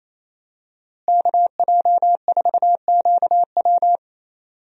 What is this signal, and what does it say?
Morse code, heard as a keyed tone, about 28 wpm, 700 Hz. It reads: KJ4QW
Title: USA Call Signs - 5 Characters - Repeated 3x - 28wpm